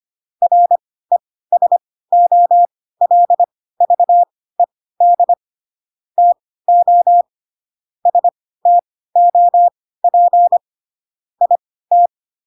Morse code, heard as a keyed tone, about 25 wpm, spaced as for 16 wpm, 700 Hz. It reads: RESOLVED TO STOP IT
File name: Pinocchio / Pinocchio ch3 10 25 ik1hge